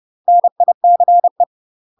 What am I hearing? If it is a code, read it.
Morse code, heard as a keyed tone, about 30 wpm, 700 Hz. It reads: NICE
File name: UK MASTER QSOs TRAINING/30WPM